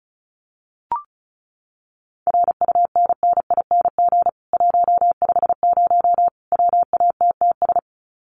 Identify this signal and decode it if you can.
Morse code, heard as a keyed tone, about 34 wpm, 700 Hz. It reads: RUNNING 150 WATTS
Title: Common QSO Phrases 35wpm